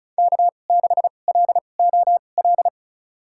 Morse code, heard as a keyed tone, about 34 wpm, spaced as for 25 wpm, 700 Hz. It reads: K6LOL
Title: USA Call Signs - 2x Character Spacing - 35wpm